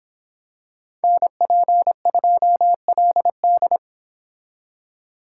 Morse code, heard as a keyed tone, about 26 wpm, 700 Hz. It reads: NP2LD